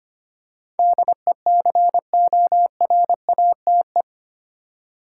Morse code, heard as a keyed tone, about 25 wpm, 700 Hz. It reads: DECORATE
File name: Eight Letter Words - 25wpm